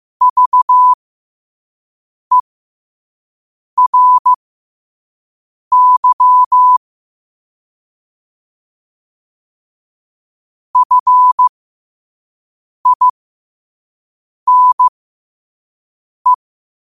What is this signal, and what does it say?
Morse code, heard as a keyed tone, about 15 wpm, spaced as for 5 wpm, 1000 Hz. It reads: VERY FINE
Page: New quotes every day in morse code at 5 Words per minute.